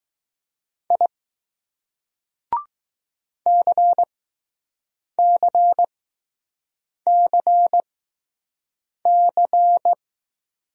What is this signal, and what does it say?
Morse code, heard as a keyed tone, about 20 wpm, spaced as for 6 wpm, 700 Hz. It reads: I CCCC